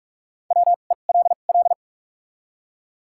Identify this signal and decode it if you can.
Morse code, heard as a keyed tone, about 46 wpm, spaced as for 32 wpm, 700 Hz. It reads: WELL
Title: 2x Character Spacing - Top 500 Words - 45wpm